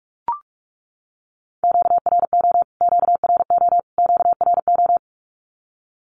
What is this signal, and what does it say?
Morse code, heard as a keyed tone, about 44 wpm, 700 Hz. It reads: QRO QRO QRO
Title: Q-Codes Repeated 3x 45wpm